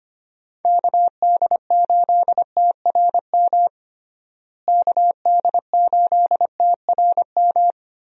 Morse code, heard as a keyed tone, about 25 wpm, 700 Hz. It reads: KD8TRM KD8TRM
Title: USA Call Signs - 6 Characters - Repeated 2x - 25wpm